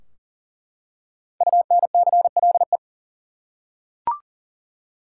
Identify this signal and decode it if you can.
Morse code, heard as a keyed tone, about 40 wpm, 700 Hz. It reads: UNCLE
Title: Five Letter Words - 40wpm